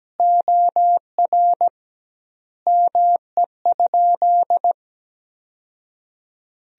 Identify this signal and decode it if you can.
Morse code, heard as a keyed tone, about 17 wpm, 700 Hz. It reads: OR ME?
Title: 2x Word-Spacing - Sentences from Top 200 Words - 17wpm